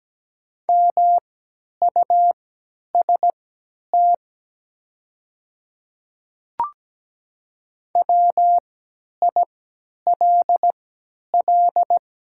Morse code, heard as a keyed tone, about 17 wpm, spaced as for 10 wpm, 700 Hz. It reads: MUST WILL
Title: 3x Character Spacing - Top 200 Words - 17wpm